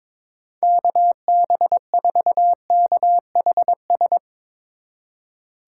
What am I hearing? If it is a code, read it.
Morse code, heard as a keyed tone, about 24 wpm, 700 Hz. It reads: KB4KHS
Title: Speed Racing - USA 2x3 Call Signs 22wpm